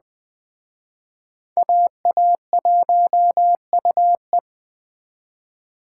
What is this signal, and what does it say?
Morse code, heard as a keyed tone, about 20 wpm, 700 Hz. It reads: AA1UE
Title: Speed Racing - USA 2x2 Call Signs 17wpm